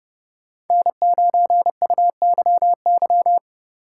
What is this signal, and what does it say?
Morse code, heard as a keyed tone, about 31 wpm, 700 Hz. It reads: N9UYY